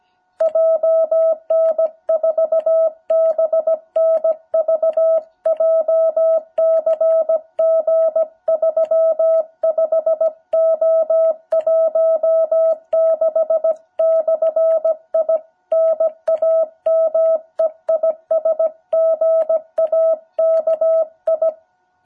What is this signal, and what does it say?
Morse code, heard as a keyed tone, about 16 wpm, 660 Hz. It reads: JN4BNV JCG35O16/I NAME IS GAKI
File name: jh8cbh / 2021cw